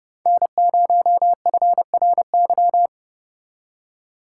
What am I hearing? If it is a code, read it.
Morse code, heard as a keyed tone, about 32 wpm, 700 Hz. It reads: N0FRY